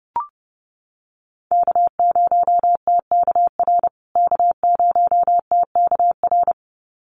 Morse code, heard as a keyed tone, about 30 wpm, 700 Hz. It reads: K0TKR K0TKR
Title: USA Call Signs Repeated 2x 30wpm